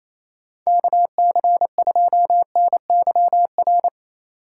Morse code, heard as a keyed tone, about 28 wpm, 700 Hz. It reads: KC2NYR
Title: USA Call Signs - 6 Characters - 2x Character Spacing - 28wpm